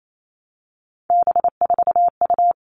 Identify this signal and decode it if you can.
Morse code, heard as a keyed tone, about 28 wpm, 700 Hz. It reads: B4U
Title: All Cut Numbers - 3-Digit 28wpm